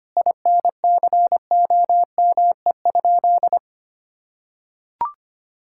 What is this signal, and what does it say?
Morse code, heard as a keyed tone, about 25 wpm, 700 Hz. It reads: INCOME?
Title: Sentences from Top 1000 Words - Review Part 5 - 25wpm